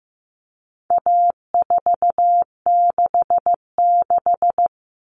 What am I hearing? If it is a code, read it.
Morse code, heard as a keyed tone, about 15 wpm, 700 Hz. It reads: A466
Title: Common Cut Numbers - 4-Digit 15wpm